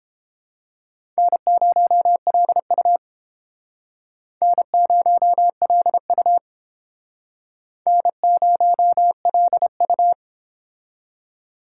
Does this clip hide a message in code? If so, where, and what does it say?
Yes: Morse code from 1.2 to 10.1 s, heard as a keyed tone, about 29 wpm, 700 Hz. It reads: N0LU N0LU N0LU